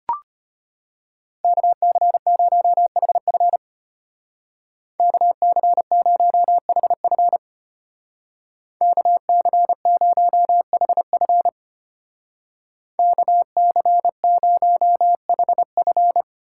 Morse code, heard as a keyed tone, about 31 wpm, 700 Hz. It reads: KC0HF KC0HF KC0HF KC0HF